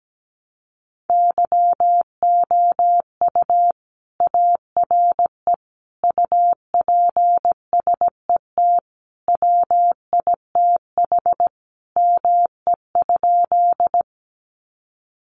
Morse code, heard as a keyed tone, about 17 wpm, 700 Hz. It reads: YOU ARE UPSET WITH ME?
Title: Sentences from Top 800 Words 17wpm